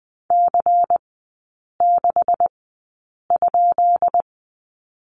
Morse code, heard as a keyed tone, about 20 wpm, spaced as for 8 wpm, 700 Hz. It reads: C6?
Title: 133 Lesson 22e - Question Mark - Three Characters 20wpm